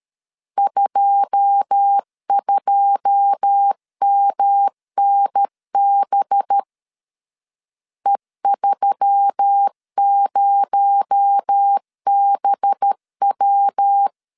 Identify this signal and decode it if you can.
Morse code, heard as a keyed tone, about 13 wpm, 800 Hz. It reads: 22MNB E30BW
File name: LEZIONI / 9 2 12wpm 2301